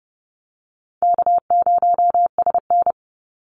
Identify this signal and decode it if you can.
Morse code, heard as a keyed tone, about 30 wpm, 700 Hz. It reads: K0SN
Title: Call Signs Repeated 3x - 30wpm